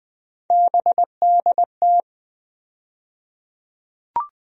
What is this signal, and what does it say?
Morse code, heard as a keyed tone, about 20 wpm, 700 Hz. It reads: BDT
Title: USA Call Sign Postfixes - 3 Letters 20wpm